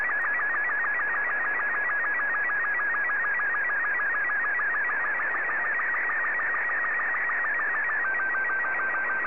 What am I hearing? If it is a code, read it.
SSTV (slow-scan television), Robot 36.
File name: sstv